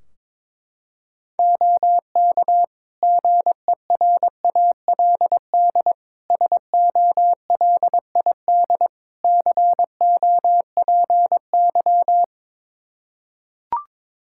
Morse code, heard as a keyed tone, about 22 wpm, 700 Hz. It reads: OK GERALD SOLID COPY
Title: Practice QSO - Exchange 3 - More Info - 22wpm